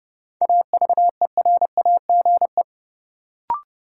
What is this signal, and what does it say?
Morse code, heard as a keyed tone, about 30 wpm, 700 Hz. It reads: AVERAGE
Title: Sets of 5 Words - New Word Drill - 30wpm